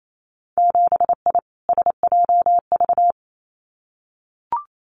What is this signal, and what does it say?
Morse code, heard as a keyed tone, about 28 wpm, 700 Hz. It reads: 7I SJV
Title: Winter Field Day Exchanges Encore 28wpm